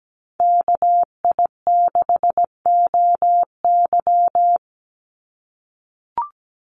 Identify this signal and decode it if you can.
Morse code, heard as a keyed tone, about 17 wpm, 700 Hz. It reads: KI6OY
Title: Call Signs - 17wpm